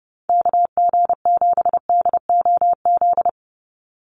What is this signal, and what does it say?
Morse code, heard as a keyed tone, about 30 wpm, 700 Hz. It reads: KG7DOZ
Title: USA Call Signs 30wpm